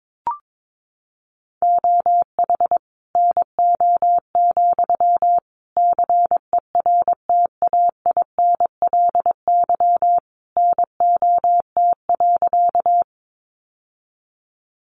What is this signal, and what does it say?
Morse code, heard as a keyed tone, about 22 wpm, 700 Hz. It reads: OH NO, CERTAINLY NOT.
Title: Sentences from Top 900 Words 22wpm